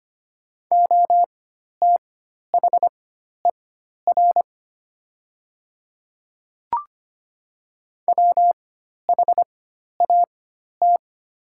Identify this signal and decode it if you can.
Morse code, heard as a keyed tone, about 25 wpm, spaced as for 12 wpm, 700 Hz. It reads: OTHER WHAT
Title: 4x Character Spacing - Top 500 Words - 25wpm